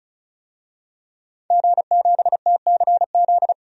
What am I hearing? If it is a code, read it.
Morse code, heard as a keyed tone, about 36 wpm, 700 Hz. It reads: G7TCZ
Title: DXCC-100 Call Signs - 5 Characters - 35wpm